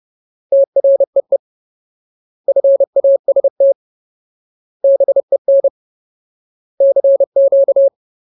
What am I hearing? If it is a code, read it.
Morse code, heard as a keyed tone, about 30 wpm, 550 Hz. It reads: TREE FAST BEN CQ
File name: Ultimate Word List / 30WPM Ultimate